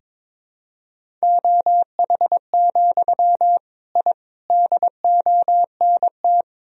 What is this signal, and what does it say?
Morse code, heard as a keyed tone, about 22 wpm, 700 Hz. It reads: OH, I DONT
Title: Sentences from Top 1000 Words - Review Part 1 - 22wpm